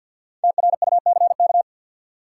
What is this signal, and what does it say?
Morse code, heard as a keyed tone, about 53 wpm, 700 Hz. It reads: TRUCK
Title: Five Letter Words - 50wpm